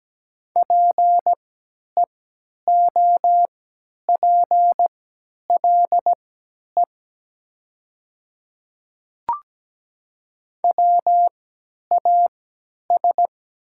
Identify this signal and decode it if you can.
Morse code, heard as a keyed tone, about 17 wpm, spaced as for 10 wpm, 700 Hz. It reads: PEOPLE WAS
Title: 3x Character Spacing - Top 200 Words - 17wpm